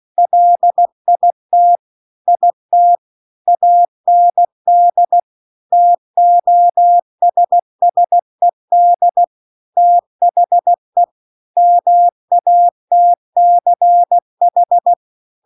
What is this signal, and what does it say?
Morse code, heard as a keyed tone, about 16 wpm, 700 Hz. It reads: LIT IT AND TOSSED THE MATCH
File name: CW-The-Big-Twoheaded-River-2-16WPM